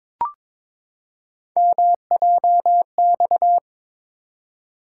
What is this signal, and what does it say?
Morse code, heard as a keyed tone, about 23 wpm, 700 Hz. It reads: MJX
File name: practice / 22wpm / Three Letter-Number Combinations - 22wpm